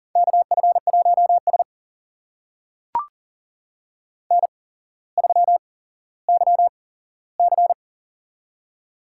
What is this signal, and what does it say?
Morse code, heard as a keyed tone, about 40 wpm, spaced as for 17 wpm, 700 Hz. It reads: KF1S N3YC
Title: USA Call Signs - 4 Characters - 8x Character Spacing - 40wpm